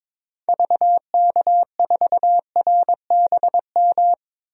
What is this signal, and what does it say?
Morse code, heard as a keyed tone, about 22 wpm, 700 Hz. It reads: VK4RBM
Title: DXCC-100 Call Signs - 6 Characters - 22wpm